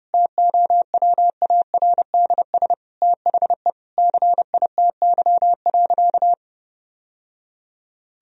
Morse code, heard as a keyed tone, about 32 wpm, 700 Hz. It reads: TOWARDS THE CITY.
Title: Sentences from Top 1000 Words - Review Part 2 - 30wpm